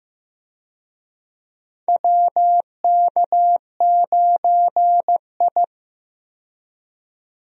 Morse code, heard as a keyed tone, about 15 wpm, 700 Hz. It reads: WK9I
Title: USA Call Signs - 4 Characters - 15wpm